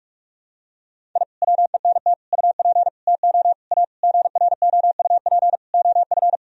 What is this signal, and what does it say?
Morse code, heard as a keyed tone, about 46 wpm, 700 Hz. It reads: I WENT UP TO A GROUP OF
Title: Sentences from Top 1000 Words - Review Part 1 - 45wpm